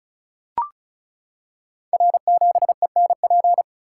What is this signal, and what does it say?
Morse code, heard as a keyed tone, about 36 wpm, 700 Hz. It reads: R7ENP
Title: Code Groups - 5 Letter:Numbers - 35wpm